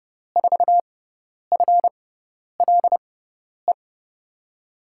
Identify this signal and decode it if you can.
Morse code, heard as a keyed tone, about 30 wpm, spaced as for 10 wpm, 700 Hz. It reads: 4FLE
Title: USA Call Signs - 6 Characters - 6x Character Spacing - 30wpm